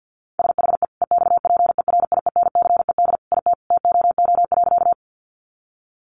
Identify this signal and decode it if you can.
Morse code, heard as a keyed tone, about 50 wpm, 700 Hz. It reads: SHE EXPERIENCED IT TOO.
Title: Sentences from Top 1100 Words 50wpm